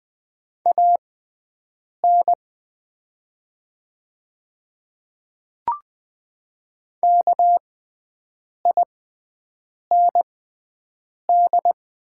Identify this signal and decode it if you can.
Morse code, heard as a keyed tone, about 20 wpm, spaced as for 7 wpm, 690 Hz. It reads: AN KIND
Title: 6x Character Spacing - Top 200 Words - 20wpm